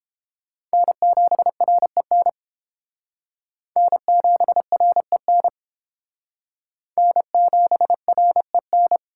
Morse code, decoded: N7REN N7REN N7REN

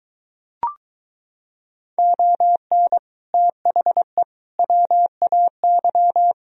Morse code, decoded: ON THE WAY

